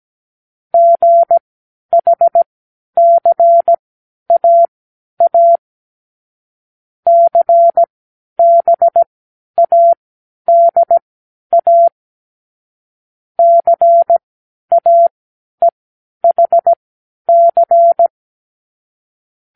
Morse code, heard as a keyed tone, about 17 wpm, spaced as for 10 wpm, 680 Hz. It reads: GHCAA CBADA CAEHC
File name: k7qo / Lesson12(A-H)